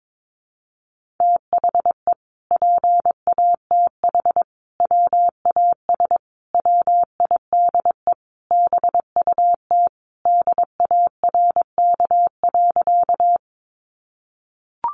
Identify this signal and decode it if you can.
Morse code, heard as a keyed tone, about 22 wpm, 700 Hz. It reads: THE PATH WAS WIDE BUT DARK.
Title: Sentences from Top 1400 Words 22wpm